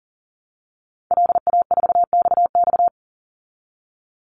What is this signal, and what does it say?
Morse code, heard as a keyed tone, about 38 wpm, 700 Hz. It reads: LA4XX
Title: Call Signs Repeated 2x - 40wpm